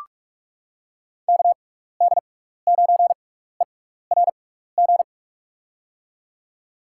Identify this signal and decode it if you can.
Morse code, heard as a keyed tone, about 46 wpm, spaced as for 16 wpm, 700 Hz. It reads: KD9ERG
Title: USA Call Signs - 6 Characters - 6x Character Spacing - 45wpm